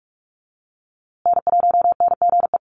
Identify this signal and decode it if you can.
Morse code, heard as a keyed tone, about 44 wpm, 700 Hz. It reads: N1NGE
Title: USA Call Signs Repeated 3x 45wpm